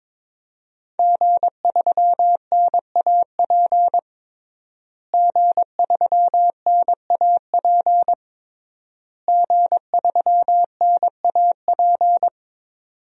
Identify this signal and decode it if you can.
Morse code, heard as a keyed tone, about 22 wpm, 700 Hz. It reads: G3NAP G3NAP G3NAP
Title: DXCC-100 Call Signs - 5 Characters - Repeated 3x - 22wpm